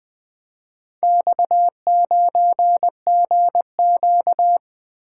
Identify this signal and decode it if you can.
Morse code, heard as a keyed tone, about 24 wpm, 700 Hz. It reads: X9GQ
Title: Speed-Racing - Four Letter-Number Combinations - 20wpm